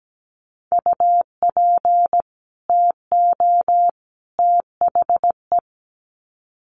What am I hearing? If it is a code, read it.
Morse code, heard as a keyed tone, about 17 wpm, 700 Hz. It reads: UP TO THE
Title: Sets of 3 Words 17wpm